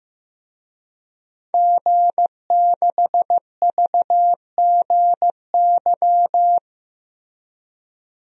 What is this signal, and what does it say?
Morse code, heard as a keyed tone, about 15 wpm, 700 Hz. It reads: G6VGY
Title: DXCC-100 Call Signs - 5 Characters - Repeated 2x - 15wpm